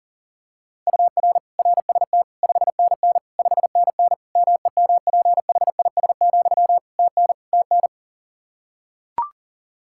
Morse code, heard as a keyed tone, about 38 wpm, 700 Hz. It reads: UR RST 5NN 5NN MEMPHIS, TN TN
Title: Practice QSO - Exchange 1 - Beginning - 40wpm